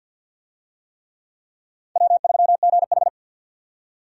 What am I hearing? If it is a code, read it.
Morse code, heard as a keyed tone, about 52 wpm, 700 Hz. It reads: W3GH
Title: USA Call Signs - 4 Characters - 50wpm